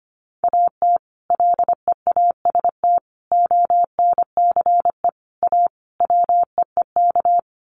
Morse code, heard as a keyed tone, about 25 wpm, 700 Hz. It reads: AT LEAST ONCE A WEEK
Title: Sets of 5 Words Encore2 25wpm